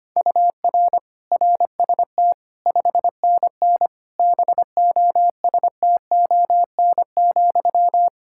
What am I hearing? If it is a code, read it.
Morse code, heard as a keyed tone, about 25 wpm, 700 Hz. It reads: UR RST 5NN BOSTON,
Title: Practice QSO - Exchange 1 - Beginning - 25wpm